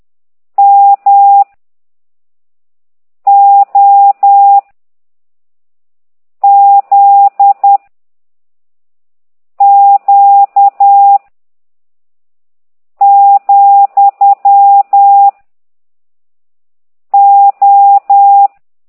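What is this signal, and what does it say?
Morse code, heard as a keyed tone, about 10 wpm, 800 Hz. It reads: M O Z Q , O
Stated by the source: Morse code lesson 5 exercise 10 by CT2IDW, containing only characters M, O, G, Z, Q and comma.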